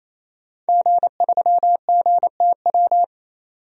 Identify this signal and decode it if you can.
Morse code, heard as a keyed tone, about 28 wpm, 700 Hz. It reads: G3GTW
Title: DXCC-100 Call Signs - 5 Characters - 28wpm